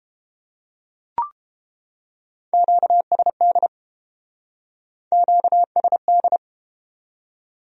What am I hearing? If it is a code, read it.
Morse code, heard as a keyed tone, about 31 wpm, 700 Hz. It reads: QSD QSD